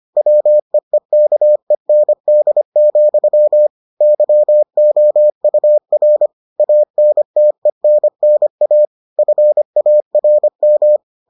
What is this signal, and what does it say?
Morse code, heard as a keyed tone, about 25 wpm, 600 Hz. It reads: WEEKEND, YOUR ANTENNA FARM